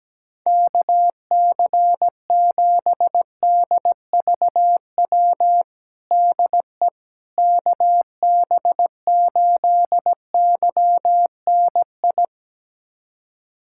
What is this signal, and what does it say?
Morse code, heard as a keyed tone, about 17 wpm, 700 Hz. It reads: KC7DVW DE KB8YNI
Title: USA Callsign DE Callsign 17wpm